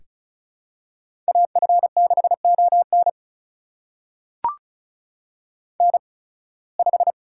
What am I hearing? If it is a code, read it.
Morse code, heard as a keyed tone, about 36 wpm, spaced as for 23 wpm, 700 Hz. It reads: AF6ON N5
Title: USA Call Signs - 8x Character Spacing - 35wpm